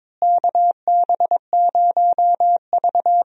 Morse code, heard as a keyed tone, about 22 wpm, 700 Hz. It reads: KB0V
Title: USA Call Signs - 4 Characters - Repeated 3x - 22wpm